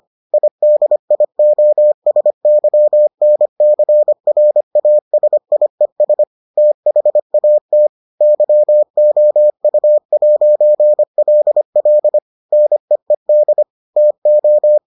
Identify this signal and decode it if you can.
Morse code, heard as a keyed tone, about 25 wpm, 600 Hz. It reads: IDIOSYNCRASIES THAT YOU'LL NEED TO